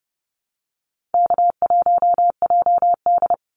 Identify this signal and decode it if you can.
Morse code, heard as a keyed tone, about 30 wpm, 700 Hz. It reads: K1JD
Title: NA SOTA Activator Callsigns 30wpm